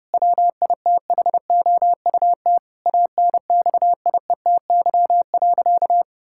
Morse code, heard as a keyed tone, about 30 wpm, 700 Hz. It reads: WITHOUT ANXIETY.
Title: Sentences from Top 1600 Words 30wpm